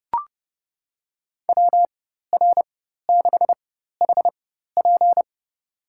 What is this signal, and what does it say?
Morse code, heard as a keyed tone, about 31 wpm, spaced as for 14 wpm, 700 Hz. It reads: WR6HP
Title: USA Call Signs - 5 Characters - 4x Character Spacing - 30wpm